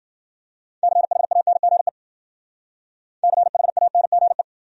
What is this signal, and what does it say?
Morse code, heard as a keyed tone, about 51 wpm, 700 Hz. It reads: CHANGE CHANGE